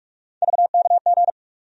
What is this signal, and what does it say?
Morse code, heard as a keyed tone, about 47 wpm, 700 Hz. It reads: VKG
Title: USA Call Sign Postfixes - 3 Letters 45wpm